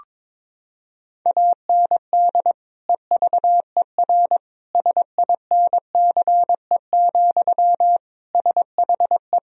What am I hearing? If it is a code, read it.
Morse code, heard as a keyed tone, about 22 wpm, 700 Hz. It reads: AND EVER SINCE, SHE
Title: Sentences from Top 1000 Words - Review Part 1 - 22wpm